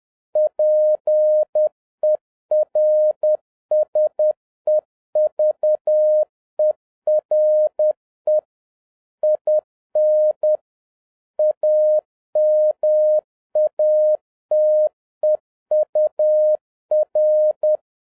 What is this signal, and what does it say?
Morse code, heard as a keyed tone, about 10 wpm, 610 Hz. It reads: PERSEVERE IN AMATEUR